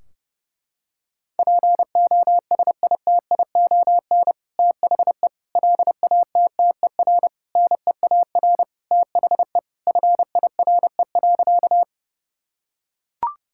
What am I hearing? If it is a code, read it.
Morse code, heard as a keyed tone, about 30 wpm, 700 Hz. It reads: POSITION THE LATTER NEAR THE FIRE.
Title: Sentences from Top 1600 Words 30wpm